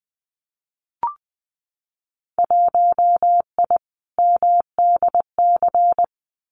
Morse code, heard as a keyed tone, about 20 wpm, 700 Hz. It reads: 1I MDC
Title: Winter Field Day Exchanges 20wpm